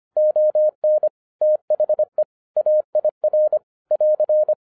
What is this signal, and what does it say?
Morse code, heard as a keyed tone, about 25 wpm, 610 Hz. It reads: ON THE AIR <AR>